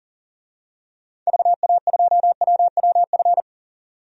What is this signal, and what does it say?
Morse code, heard as a keyed tone, about 40 wpm, 700 Hz. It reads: VA2WWF
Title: DXCC-100 Call Signs - 6 Characters - 40wpm